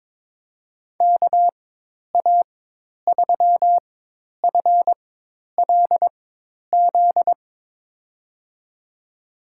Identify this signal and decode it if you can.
Morse code, heard as a keyed tone, about 22 wpm, spaced as for 10 wpm, 700 Hz. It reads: KA3FLZ
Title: USA Call Signs - 6 Characters - 4x Character Spacing - 22wpm